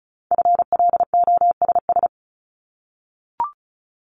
Morse code, heard as a keyed tone, about 36 wpm, 700 Hz. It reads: FLOSS
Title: Multi-letter Phonemes 35wpm